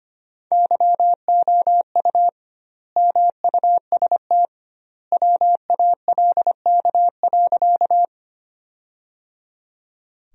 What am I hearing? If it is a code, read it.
Morse code, heard as a keyed tone, about 25 wpm, 700 Hz. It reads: YOU MUST WALK.
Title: 2x Word-Spacing - Sentences from Top 300 Words - 25wpm